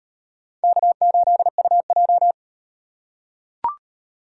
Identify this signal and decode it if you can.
Morse code, heard as a keyed tone, about 38 wpm, 700 Hz. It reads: K8UJ